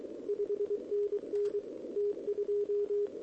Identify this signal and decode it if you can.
Morse code, heard as a keyed tone, about 23 wpm, 410 Hz. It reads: 5NN T2
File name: OK QRP 2026 / OKQRP2026